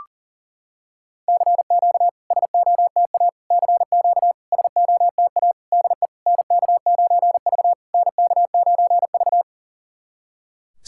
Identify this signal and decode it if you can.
Morse code, heard as a keyed tone, about 40 wpm, 700 Hz. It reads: CQ SOTA CQ SOTA DE NK9V NK9V
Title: SOTA QSO Practice 40wpm